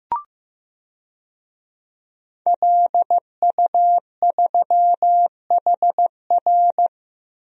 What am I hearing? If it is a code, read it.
Morse code, heard as a keyed tone, about 15 wpm, 700 Hz. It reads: LU3HR
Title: DXCC-100 Call Signs - 5 Characters - 15wpm